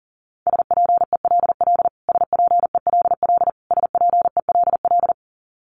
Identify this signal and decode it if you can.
Morse code, heard as a keyed tone, about 40 wpm, 700 Hz. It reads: SPELL SPELL SPELL
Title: Top 500 Words Repeated 40wpm